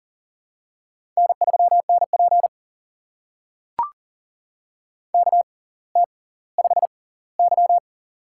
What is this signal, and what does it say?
Morse code, heard as a keyed tone, about 38 wpm, spaced as for 20 wpm, 700 Hz. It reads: N3NP KT5Y
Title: USA Call Signs - 4 Characters - 6x Character Spacing - 40wpm